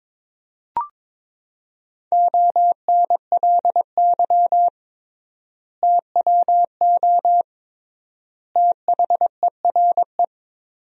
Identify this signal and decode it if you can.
Morse code, heard as a keyed tone, about 22 wpm, 700 Hz. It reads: ONLY TWO THERE
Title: 3x Word-Spacing - Sentences from Top 200 Words - 22wpm